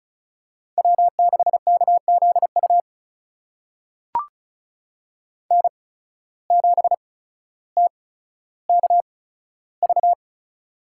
Morse code, heard as a keyed tone, about 36 wpm, spaced as for 15 wpm, 700 Hz. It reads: W6KZU N7TKV
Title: USA Call Signs - 5 Characters - 8x Character Spacing - 35wpm